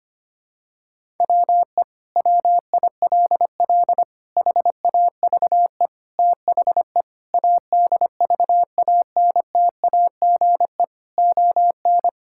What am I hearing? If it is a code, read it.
Morse code, heard as a keyed tone, about 25 wpm, 700 Hz. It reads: WE WILL HAVE THE ADVANTAGE ON
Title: Sentences from Top 1000 Words - Review Part 6 - 25wpm